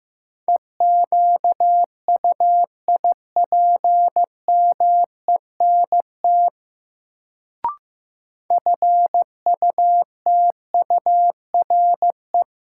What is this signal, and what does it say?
Morse code, heard as a keyed tone, about 15 wpm, 700 Hz. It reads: EQUIPMENT FUTURE
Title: Sentences from Top 700 Words - New Word Drill - 15wpm